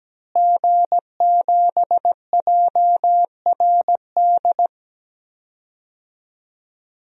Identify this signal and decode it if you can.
Morse code, heard as a keyed tone, about 17 wpm, 700 Hz. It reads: G7JRD
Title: DXCC-100 Call Signs - 5 Characters - Repeated 3x - 17wpm